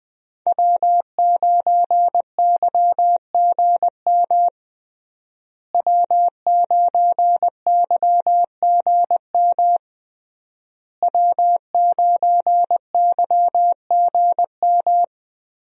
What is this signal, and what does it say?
Morse code, heard as a keyed tone, about 20 wpm, 700 Hz. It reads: W9YGM W9YGM W9YGM
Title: USA Call Signs - 5 Characters - Repeated 3x - 20wpm